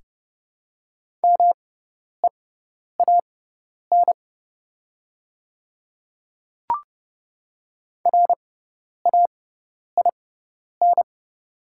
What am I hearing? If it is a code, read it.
Morse code, heard as a keyed tone, about 32 wpm, spaced as for 10 wpm, 700 Hz. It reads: MEAN RAIN
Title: 6x Character Spacing - Top 300 Words - 30wpm